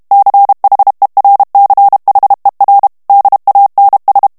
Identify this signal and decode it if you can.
Morse code, heard as a keyed tone, about 32 wpm, 800 Hz. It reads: CHERCHER DANS
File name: FRANCAIS TEXTE CLAIR 2 VITESSE 30 MOTS MINUTES